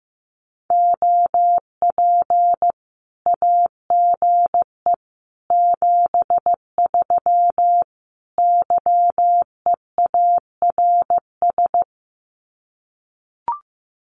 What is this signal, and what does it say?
Morse code, heard as a keyed tone, about 15 wpm, 700 Hz. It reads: OP AGE 73 YEARS
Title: Common QSO Phrases 15wpm